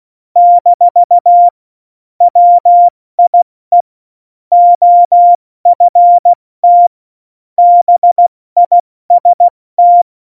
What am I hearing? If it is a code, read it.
Morse code, heard as a keyed tone, about 16 wpm, 700 Hz. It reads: - WIE OFT BIST
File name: Morsekurs / Witze / Witz 18 August 2020, 70 F80